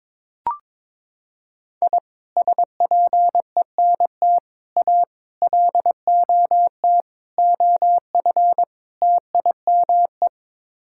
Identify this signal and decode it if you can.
Morse code, heard as a keyed tone, about 21 wpm, 700 Hz. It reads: I SPENT A LOT OF TIME
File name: practice / 22wpm / Sets of 6 Words - 22wpm